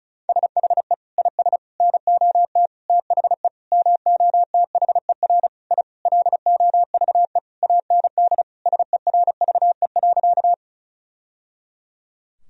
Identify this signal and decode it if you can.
Morse code, heard as a keyed tone, about 36 wpm, 700 Hz. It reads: SHE IS NOT THE MOTHER I LOVE AND SERVE.
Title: Sentences from Top 1000 Words - Review Part 2 - 35wpm